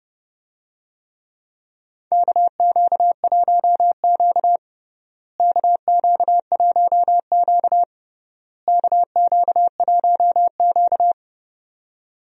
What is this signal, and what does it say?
Morse code, heard as a keyed tone, about 30 wpm, 700 Hz. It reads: KQ1Q KQ1Q KQ1Q
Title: USA Call Signs - 4 Characters - Repeated 3x - 30wpm